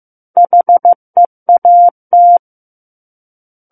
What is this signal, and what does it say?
Morse code, heard as a keyed tone, about 15 wpm, 700 Hz. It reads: HEAT
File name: Chapter 6 - 15 WPM